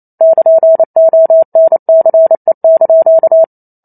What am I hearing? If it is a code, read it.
Morse code, heard as a keyed tone, about 28 wpm, 650 Hz. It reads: <KN>ONCE)
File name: Chapter 2 - 28 WPM